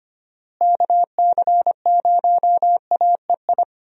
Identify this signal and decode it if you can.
Morse code, heard as a keyed tone, about 25 wpm, 700 Hz. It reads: KC0AEI
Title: USA Call Signs - 6 Characters - Repeated 2x - 25wpm